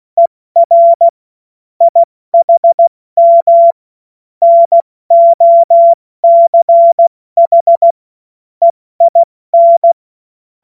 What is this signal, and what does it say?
Morse code, heard as a keyed tone, about 16 wpm, 680 Hz. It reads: ER IHM NOCH EIN
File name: Morsekurs / Witze / Witz 20. Dezember 2022 70 F80